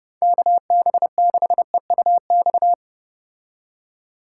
Morse code, heard as a keyed tone, about 34 wpm, 700 Hz. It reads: KB6EUX